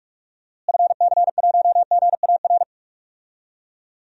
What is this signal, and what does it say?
Morse code, heard as a keyed tone, about 41 wpm, 700 Hz. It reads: FC1GAR
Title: DXCC-100 Call Signs - 6 Characters - Repeated 3x - 45wpm